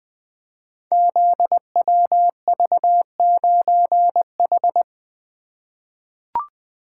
Morse code, heard as a keyed tone, about 20 wpm, 700 Hz. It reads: ZWV9H
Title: Code Groups - 5 Letter:Numbers - 20wpm